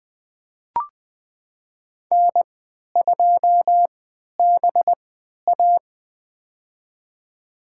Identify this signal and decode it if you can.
Morse code, heard as a keyed tone, about 20 wpm, spaced as for 11 wpm, 700 Hz. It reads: N2BA
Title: USA Call Signs - 3x Character Spacing - 20wpm